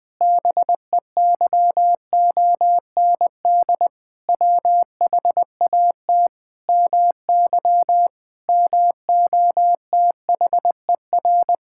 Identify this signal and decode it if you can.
Morse code, heard as a keyed tone, about 20 wpm, 700 Hz. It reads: BEYOND WHAT MY MOTHER
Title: Sentences from Top 1000 Words - Review Part 6 - 20wpm